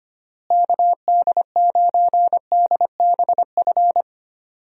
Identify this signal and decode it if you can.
Morse code, heard as a keyed tone, about 25 wpm, 700 Hz. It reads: KD9DBF
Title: USA Call Signs - 6 Characters - Repeated 2x - 25wpm